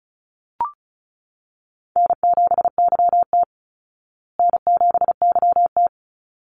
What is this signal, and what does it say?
Morse code, heard as a keyed tone, about 34 wpm, 700 Hz. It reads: N7YT N7YT
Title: Call Signs Repeated 3x - 35wpm